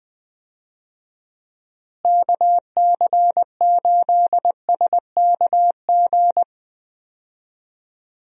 Morse code, heard as a keyed tone, about 20 wpm, 700 Hz. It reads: KC8SKG
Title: USA Call Signs - 6 Characters - Repeated 2x - 20wpm